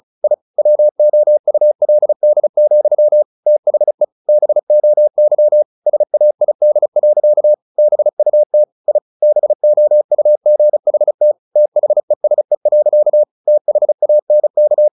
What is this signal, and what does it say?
Morse code, heard as a keyed tone, about 34 wpm, 600 Hz. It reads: I WOULD, THE BOY SAID. BUT I BOUGHT THESE. THANK
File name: the old man and the sea / the old man and the sea 1 35wpm